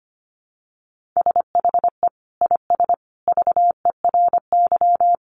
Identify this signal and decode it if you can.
Morse code, heard as a keyed tone, about 25 wpm, 700 Hz. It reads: SHE IS VERY
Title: Sentences from Top 1400 Words 25wpm